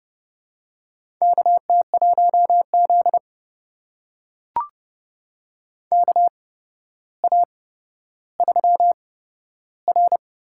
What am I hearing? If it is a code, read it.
Morse code, heard as a keyed tone, about 30 wpm, spaced as for 13 wpm, 700 Hz. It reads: KT1Z KA3R
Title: USA Call Signs - 4 Characters - 8x Character Spacing - 30wpm